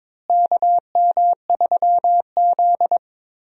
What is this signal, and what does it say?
Morse code, heard as a keyed tone, about 22 wpm, 700 Hz. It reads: KM3Z
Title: USA Call Signs - 4 Characters - Repeated 2x - 22wpm